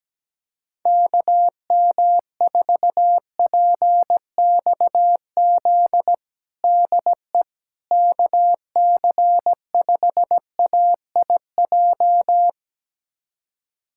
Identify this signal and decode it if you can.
Morse code, heard as a keyed tone, about 17 wpm, 700 Hz. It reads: KM4PXZ DE KC5AIJ
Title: USA Callsign DE Callsign - 6 Characters - 17wpm